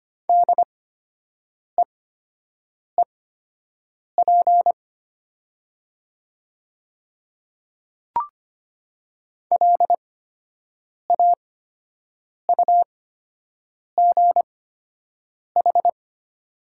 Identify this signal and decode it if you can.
Morse code, heard as a keyed tone, about 25 wpm, spaced as for 7 wpm, 700 Hz. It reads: DEEP LAUGH
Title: 8x Character Spacing - Top 500 Words - 25wpm